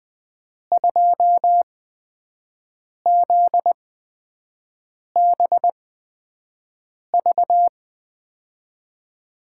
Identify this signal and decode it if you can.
Morse code, heard as a keyed tone, about 20 wpm, spaced as for 5 wpm, 700 Hz. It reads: 2ZBV
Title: USA Call Signs - 8x Character Spacing - 20wpm